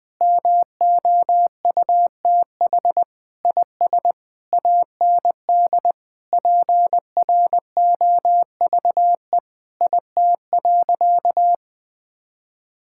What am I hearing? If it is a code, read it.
Morse code, heard as a keyed tone, about 21 wpm, 700 Hz. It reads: MOUTH IS AND PROVE IT.
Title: Sentences with 8-Word Phrases - Spoken Beginning - 20wpm